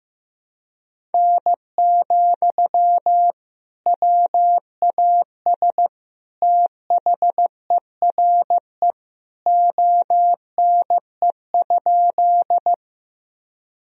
Morse code, heard as a keyed tone, about 15 wpm, 700 Hz. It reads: N, WAS THERE ONE?
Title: Sentences from Top 100 Words and Abbreviations 15wpm